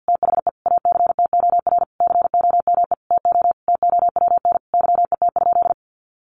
Morse code, heard as a keyed tone, about 50 wpm, 700 Hz. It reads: THE ACTOR COME TO TOWN YET?
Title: Sentences from Top 1500 Words 50wpm